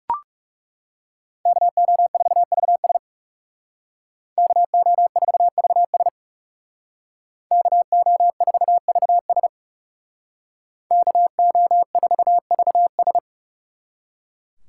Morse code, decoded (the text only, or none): KO4VS KO4VS KO4VS KO4VS